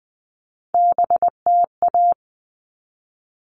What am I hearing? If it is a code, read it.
Morse code, heard as a keyed tone, about 20 wpm, 700 Hz. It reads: BTA
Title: All Cut Numbers - 3-Digit 20wpm